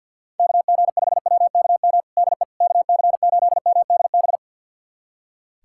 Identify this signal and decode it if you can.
Morse code, heard as a keyed tone, about 47 wpm, 700 Hz. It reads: KG5WKM DE KC8MDB
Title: USA Callsign DE Callsign 50wpm